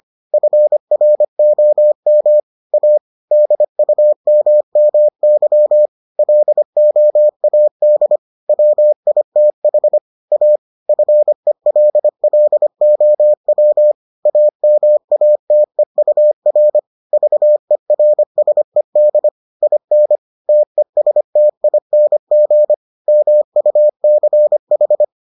Morse code, heard as a keyed tone, about 25 wpm, 600 Hz. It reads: FROM A DUMMY LOAD WITH A FELLOW AMATEUR VERSED IN TESTING MUCH